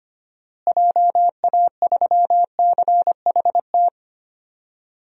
Morse code, heard as a keyed tone, about 25 wpm, 700 Hz. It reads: JA3CHT
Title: DXCC-100 Call Signs - 6 Characters - Repeated 3x - 25wpm